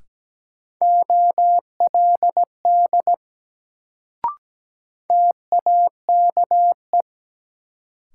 Morse code, heard as a keyed tone, about 17 wpm, 700 Hz. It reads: OLD TAKE
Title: Sentences from Top 200 Words - New Word Drill - 17wpm